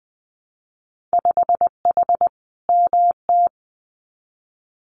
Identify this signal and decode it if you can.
Morse code, heard as a keyed tone, about 20 wpm, 700 Hz. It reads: 5H MT
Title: Winter Field Day Exchanges Encore 20wpm